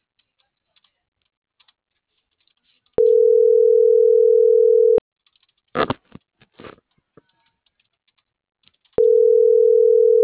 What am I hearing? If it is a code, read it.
Ringback tone (North America), 440 + 480 Hz.
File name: CENTRAL NAVIERAS / out oscar.monta.o